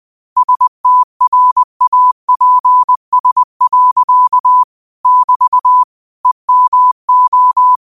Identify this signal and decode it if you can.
Morse code, heard as a keyed tone, about 20 wpm, 1000 Hz. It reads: STRAPS. <BT> EMO